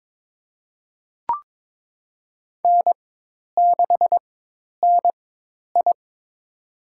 Morse code, heard as a keyed tone, about 22 wpm, spaced as for 10 wpm, 700 Hz. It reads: N6NI
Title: USA Call Signs - 4x Character Spacing - 22wpm